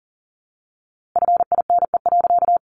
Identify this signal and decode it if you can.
Morse code, heard as a keyed tone, about 40 wpm, 700 Hz. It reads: FINE.
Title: A Sound Of Thunder 40wpm